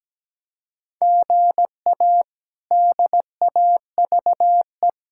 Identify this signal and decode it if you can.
Morse code, heard as a keyed tone, about 17 wpm, 700 Hz. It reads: GA DAVE
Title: K1USN SST QSO Practice - 17wpm